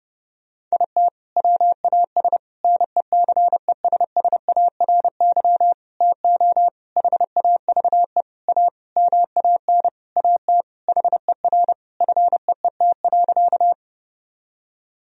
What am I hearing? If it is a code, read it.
Morse code, heard as a keyed tone, about 32 wpm, 700 Hz. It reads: IT WAS NECESSARY TO HAVE A MAN AT HER FEET.
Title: Sentences from Top 1000 Words - Review Part 1 - 30wpm